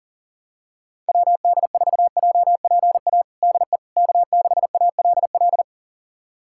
Morse code, heard as a keyed tone, about 40 wpm, 700 Hz. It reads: WD4JPA DE K6ALL
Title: USA Callsign DE Callsign 40wpm